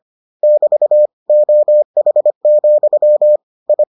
Morse code, heard as a keyed tone, about 25 wpm, 600 Hz. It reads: <BT> OH, I